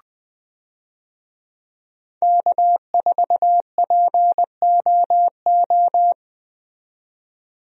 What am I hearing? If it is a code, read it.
Morse code, heard as a keyed tone, about 20 wpm, 700 Hz. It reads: K4POO
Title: USA Call Signs - 5 Characters - 20wpm